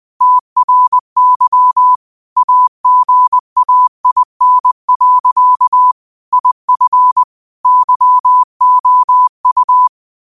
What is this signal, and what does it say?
Morse code, heard as a keyed tone, about 20 wpm, 1000 Hz. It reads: TRY AGAIN. IF YOU
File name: qotdfiles / 20wpm